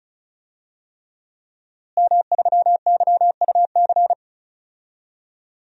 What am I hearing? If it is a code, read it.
Morse code, heard as a keyed tone, about 36 wpm, 700 Hz. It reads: M3YUC
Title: DXCC-100 Call Signs - 5 Characters - Repeated 3x - 35wpm